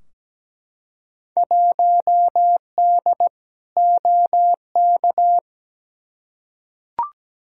Morse code, heard as a keyed tone, about 17 wpm, 700 Hz. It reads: 1D OK
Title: Field Day Exchanges - 17wpm